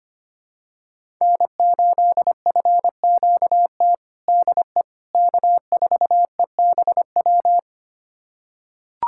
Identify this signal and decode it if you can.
Morse code, heard as a keyed tone, about 25 wpm, 700 Hz. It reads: N8FQT DE K4EBW
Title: USA Callsign DE Callsign - 5 Characters - 25wpm